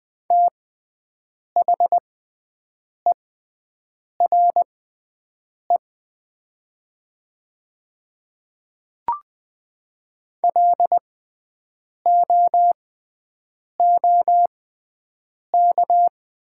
Morse code, heard as a keyed tone, about 20 wpm, spaced as for 7 wpm, 700 Hz. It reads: THERE LOOK
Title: 6x Character Spacing - Top 200 Words - 20wpm